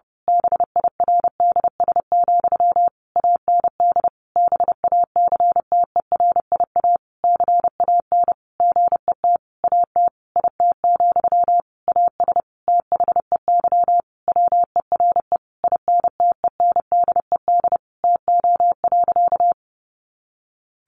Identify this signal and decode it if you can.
Morse code, heard as a keyed tone, about 30 wpm, 700 Hz. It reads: BIRDS, AND BACTERIA CAN GET AT IT, AS THEY WERE INTENDED TO.
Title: A Sound Of Thunder 30wpm